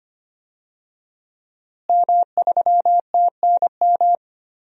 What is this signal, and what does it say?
Morse code, heard as a keyed tone, about 25 wpm, 700 Hz. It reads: M3TNM
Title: DXCC-100 Call Signs - 5 Characters - Repeated 2x - 25wpm